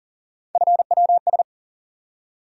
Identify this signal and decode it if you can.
Morse code, heard as a keyed tone, about 37 wpm, 700 Hz. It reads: FWS
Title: USA Call Sign Postfixes - 3 Letters 40wpm